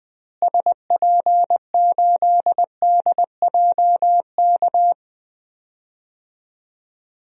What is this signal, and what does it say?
Morse code, heard as a keyed tone, about 20 wpm, 700 Hz. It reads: SP8DJK
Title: DXCC-100 Call Signs - 6 Characters - Repeated 3x - 20wpm